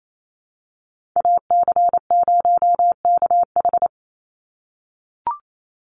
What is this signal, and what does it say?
Morse code, heard as a keyed tone, about 28 wpm, 700 Hz. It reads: AC0KH
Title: USA Call Signs 28wpm